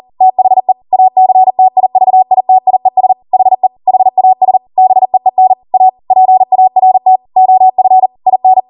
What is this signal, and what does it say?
Morse code, heard as a keyed tone, about 40 wpm, 750 Hz. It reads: THE ACTIVITIES HE HAS BEEN A PART OF IN